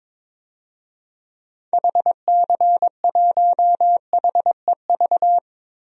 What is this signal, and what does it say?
Morse code, heard as a keyed tone, about 22 wpm, 700 Hz. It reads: HC1HEV
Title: DXCC-100 Call Signs - 6 Characters - 22wpm